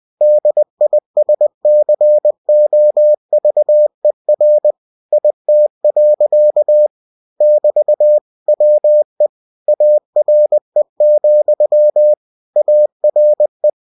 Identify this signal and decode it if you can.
Morse code, heard as a keyed tone, about 20 wpm, 600 Hz. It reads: DISCOVER IT. <BT> WE ARE, ARE